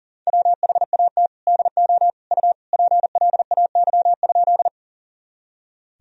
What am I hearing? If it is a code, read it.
Morse code, heard as a keyed tone, about 40 wpm, 700 Hz. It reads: WHAT DO U PLAY?
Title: Sentences from Top 200 Words and Abbreviations 40wpm